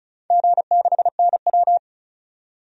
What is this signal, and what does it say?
Morse code, heard as a keyed tone, about 34 wpm, 700 Hz. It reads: G6NW
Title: DXCC-100 Call Signs - 4 Characters - 35wpm